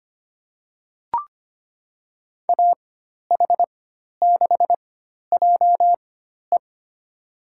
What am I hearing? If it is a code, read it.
Morse code, heard as a keyed tone, about 25 wpm, spaced as for 12 wpm, 700 Hz. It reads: AH6JE
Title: USA Call Signs - 5 Characters - 4x Character Spacing - 25wpm